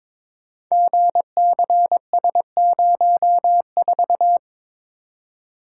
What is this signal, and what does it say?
Morse code, heard as a keyed tone, about 22 wpm, 700 Hz. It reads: GCS04
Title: Code Groups - 5 Letter:Numbers - 22wpm